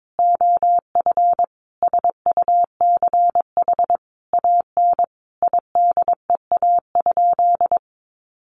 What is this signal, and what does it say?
Morse code, heard as a keyed tone, about 22 wpm, 700 Hz. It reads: OF SUCH AN IDEA?
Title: Sentences from Top 400 Words 22wpm